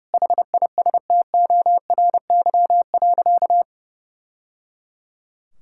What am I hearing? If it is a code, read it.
Morse code, heard as a keyed tone, about 30 wpm, 700 Hz. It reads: HISTORY.
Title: Sentences from Top 1000 Words - Review Part 5 - 30wpm